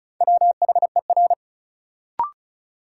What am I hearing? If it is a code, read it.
Morse code, heard as a keyed tone, about 34 wpm, 700 Hz. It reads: WHER
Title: Sentences from Top 1500 Words - New Word Drill - 35wpm